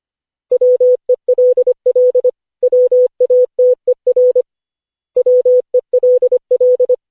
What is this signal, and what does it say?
Morse code, heard as a keyed tone, about 25 wpm, 500 Hz. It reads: WELL WATER WELL
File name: cw / Session5 cw training